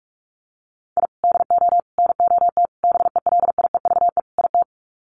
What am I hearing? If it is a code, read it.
Morse code, heard as a keyed tone, about 44 wpm, 700 Hz. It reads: I DO NOT BELIEVE IT
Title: Sets of 5 Words 45wpm